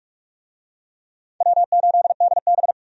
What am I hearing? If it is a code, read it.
Morse code, heard as a keyed tone, about 45 wpm, 700 Hz. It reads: W8DB
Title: USA Call Signs - 4 Characters - 45wpm